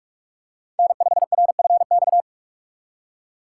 Morse code, heard as a keyed tone, about 42 wpm, 700 Hz. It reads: N5RFX
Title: USA Call Signs - 5 Characters - 3x Character Spacing - 45wpm